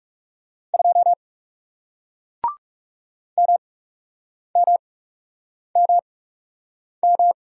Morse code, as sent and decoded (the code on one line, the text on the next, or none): ..--- / -- -- -- --
2 MMMM